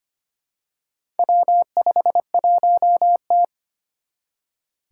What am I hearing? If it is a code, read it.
Morse code, heard as a keyed tone, about 25 wpm, 700 Hz. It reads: W51T
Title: Four Challenging Letter-Number Combinations 25wpm